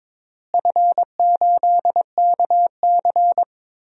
Morse code, heard as a keyed tone, about 22 wpm, 700 Hz. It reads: F8KC
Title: DXCC-100 Call Signs - 4 Characters - Repeated 3x - 22wpm